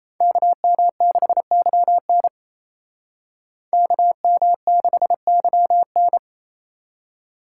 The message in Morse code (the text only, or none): KM6YN KM6YN